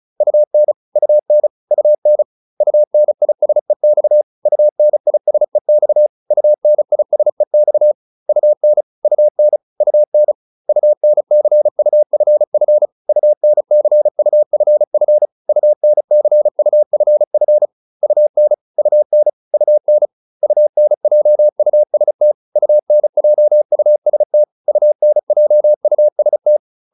Morse code, heard as a keyed tone, about 34 wpm, 600 Hz. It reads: UN UN UN UNISEX UNISEX UNISEX UN UN UN UNCUFF UNCUFF UNCUFF UN UN UN UNJUST UNJUST UNJUST